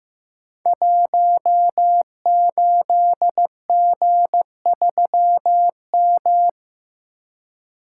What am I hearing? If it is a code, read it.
Morse code, heard as a keyed tone, about 15 wpm, 700 Hz. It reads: 18G3M
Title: Code Groups - 5 Letter:Numbers - 15wpm